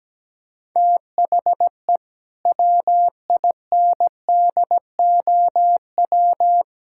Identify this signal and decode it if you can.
Morse code, heard as a keyed tone, about 17 wpm, 700 Hz. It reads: THE WINDOW
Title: Sentences from Top 1600 Words 17wpm